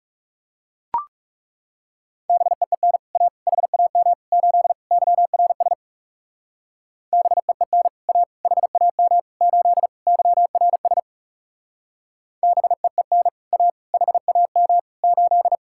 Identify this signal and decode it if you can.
Morse code, heard as a keyed tone, about 39 wpm, 700 Hz. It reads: BEEN A HAM 8 YRS BEEN A HAM 8 YRS BEEN A HAM 8